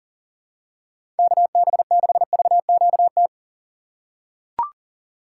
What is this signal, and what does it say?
Morse code, heard as a keyed tone, about 40 wpm, 700 Hz. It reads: KB6VQT
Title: USA Call Signs - 6 Characters - 3x Character Spacing - 40wpm